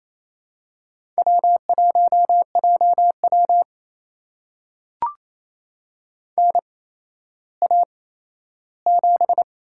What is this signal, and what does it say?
Morse code, heard as a keyed tone, about 28 wpm, spaced as for 14 wpm, 700 Hz. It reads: W1JW NA7
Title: USA Call Signs - 4 Characters - 8x Character Spacing - 28wpm